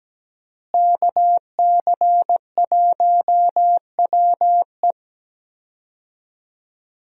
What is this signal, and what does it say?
Morse code, heard as a keyed tone, about 17 wpm, 700 Hz. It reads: KC1WE
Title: USA Call Signs - 5 Characters - Repeated 2x - 17wpm